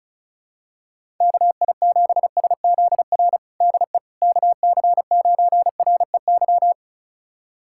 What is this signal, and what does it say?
Morse code, heard as a keyed tone, about 34 wpm, 700 Hz. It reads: KI7SZR DE KC9REY
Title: USA Callsign DE Callsign - 6 Characters - 35wpm